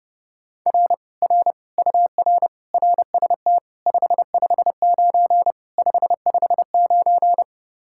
Morse code, heard as a keyed tone, about 30 wpm, 700 Hz. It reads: R R UR RST 559 559